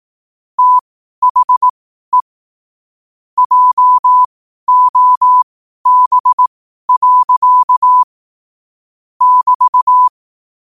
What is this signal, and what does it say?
Morse code, heard as a keyed tone, about 18 wpm, spaced as for 12 wpm, 1000 Hz. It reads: THE JOB. <BT>
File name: qotdfiles / 12wpm